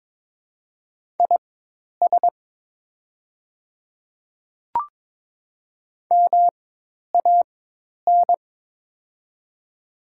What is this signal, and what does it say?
Morse code, heard as a keyed tone, about 22 wpm, spaced as for 10 wpm, 700 Hz. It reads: IS MAN
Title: 4x Character Spacing - Top 300 Words - 22wpm